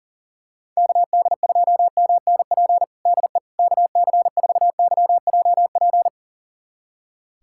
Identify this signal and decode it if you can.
Morse code, heard as a keyed tone, about 40 wpm, 700 Hz. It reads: KD2MNP DE KC4YJP
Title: USA Callsign DE Callsign 40wpm